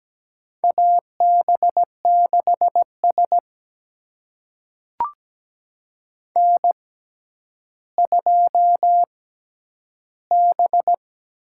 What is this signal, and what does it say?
Morse code, heard as a keyed tone, about 17 wpm, spaced as for 10 wpm, 700 Hz. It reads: AB6S N2B
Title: USA Call Signs - 4 Characters - 6x Character Spacing - 17wpm